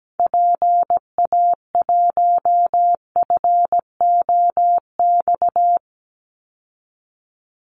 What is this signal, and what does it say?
Morse code, heard as a keyed tone, about 17 wpm, 700 Hz. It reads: PA1FOX
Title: Call Signs Repeated 2x - 17wpm